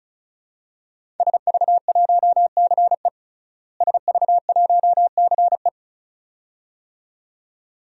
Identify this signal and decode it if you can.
Morse code, heard as a keyed tone, about 36 wpm, 700 Hz. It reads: SV1CE SV1CE
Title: DXCC-100 Call Signs - 5 Characters - Repeated 2x - 35wpm